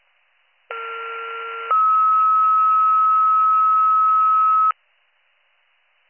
Two-tone page, heard as a one-second tone, then a three-second tone, about 1370 then 1290 Hz. It is heard from 0.7 to 4.7 s.